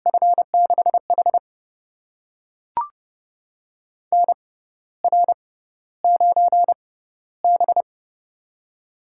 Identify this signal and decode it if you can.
Morse code, heard as a keyed tone, about 30 wpm, spaced as for 14 wpm, 700 Hz. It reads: F6H NR9B
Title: USA Call Signs - 4 Characters - 6x Character Spacing - 30wpm